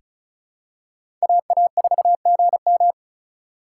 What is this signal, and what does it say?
Morse code, heard as a keyed tone, about 36 wpm, 700 Hz. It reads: AA4GM
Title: USA Call Signs - 4x Character Spacing - 35wpm